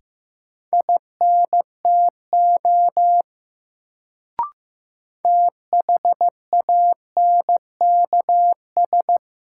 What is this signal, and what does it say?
Morse code, heard as a keyed tone, about 15 wpm, 700 Hz. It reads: INTO THANKS
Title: Sentences from Top 600 Words - New Word Drill - 15wpm